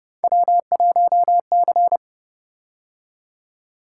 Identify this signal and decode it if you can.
Morse code, heard as a keyed tone, about 29 wpm, 700 Hz. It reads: W1C